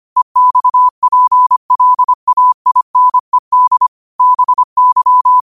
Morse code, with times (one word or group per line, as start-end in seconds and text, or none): 0.2-3.9 EXPLAINED
4.2-5.4 BY